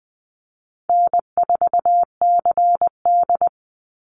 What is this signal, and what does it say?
Morse code, heard as a keyed tone, about 20 wpm, 700 Hz. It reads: N4CD
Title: Call Signs Repeated 2x - 20wpm